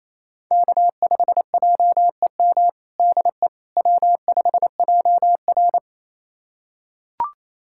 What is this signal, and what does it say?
Morse code, heard as a keyed tone, about 28 wpm, 700 Hz. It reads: K5JEM DE W5JR
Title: USA Callsign DE Callsign 28wpm